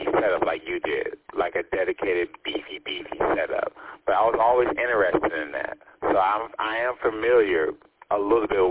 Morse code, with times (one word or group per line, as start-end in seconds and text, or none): none